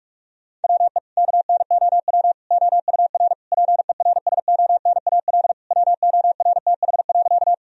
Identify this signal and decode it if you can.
Morse code, heard as a keyed tone, about 42 wpm, 700 Hz. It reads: WE KNOW OUR PERSONAL WORTH.
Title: Sentences from Top 1000 Words - Review Part 4 - 45wpm